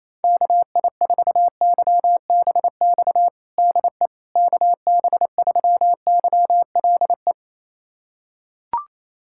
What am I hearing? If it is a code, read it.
Morse code, heard as a keyed tone, about 27 wpm, 700 Hz. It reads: KI4YBX DE KB3YLE
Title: USA Callsign DE Callsign - 6 Characters - 28wpm